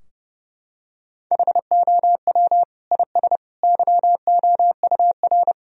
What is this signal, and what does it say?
Morse code, heard as a keyed tone, about 30 wpm, 700 Hz. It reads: HOW IS YOUR
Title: Sentences from Top 1000 Words - Review Part 3 - 30wpm